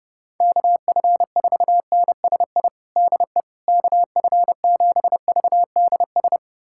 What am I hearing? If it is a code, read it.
Morse code, heard as a keyed tone, about 30 wpm, 700 Hz. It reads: KF4NSI DE KF7VDS
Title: USA Callsign DE Callsign - 6 Characters - 30wpm